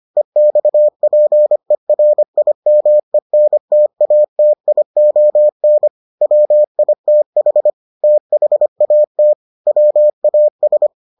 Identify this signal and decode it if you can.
Morse code, heard as a keyed tone, about 25 wpm, 600 Hz. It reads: EXPERIMENTATION WITH THAT WAS